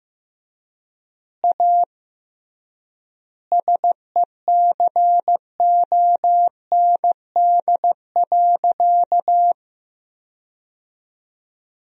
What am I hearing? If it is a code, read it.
Morse code, heard as a keyed tone, about 15 wpm, 700 Hz. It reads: A SECOND.
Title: 3x Word-Spacing - Sentences from Top 500 Words - 15wpm